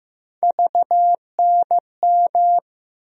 Morse code, heard as a keyed tone, about 15 wpm, 700 Hz. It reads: VNM
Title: USA Call Sign Postfixes - 3 Letters 15wpm